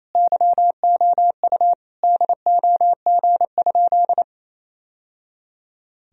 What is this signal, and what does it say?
Morse code, heard as a keyed tone, about 27 wpm, 700 Hz. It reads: YOU DOG?
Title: Sentences from Top 1000 Words - Review Part 2 - 28wpm